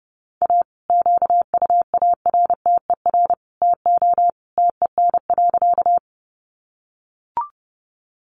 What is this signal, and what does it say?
Morse code, heard as a keyed tone, about 30 wpm, 700 Hz. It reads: A QUARTER TO TEN.
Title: Sentences from Top 1300 Words 30wpm